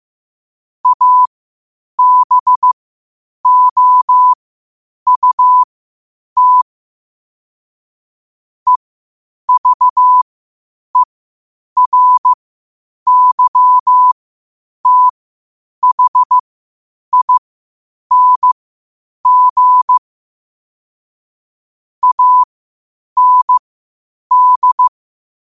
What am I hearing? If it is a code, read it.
Morse code, heard as a keyed tone, about 15 wpm, spaced as for 8 wpm, 1000 Hz. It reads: ABOUT EVERYTHING AND
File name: qotdfiles / 8wpm